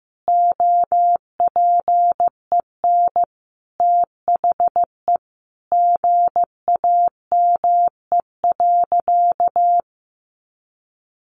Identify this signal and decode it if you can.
Morse code, heard as a keyed tone, about 15 wpm, 700 Hz. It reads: OPEN THE GAME.
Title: Sentences from Top 1500 Words 15wpm